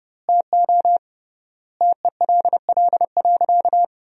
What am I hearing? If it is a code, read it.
Morse code, heard as a keyed tone, about 30 wpm, 700 Hz. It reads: TO TELL.
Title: 3x Word-Spacing - Sentences from Top 200 Words - 30wpm